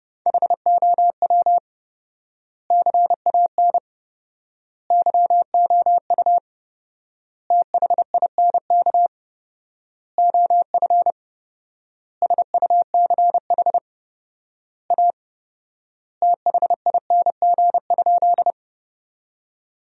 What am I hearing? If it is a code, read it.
Morse code, heard as a keyed tone, about 30 wpm, 700 Hz. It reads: HOW CAN YOU THINK OF SUCH A THING?
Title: 4x Word-Spacing - Sentences from Top 200 Words - 30wpm